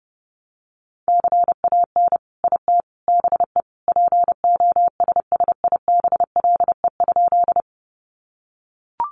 Morse code, heard as a keyed tone, about 30 wpm, 700 Hz. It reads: CAN IT BE POSSIBLE?
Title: Sentences from Top 500 Words 30wpm